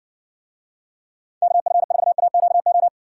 Morse code, heard as a keyed tone, about 58 wpm, 700 Hz. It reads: KF4AQO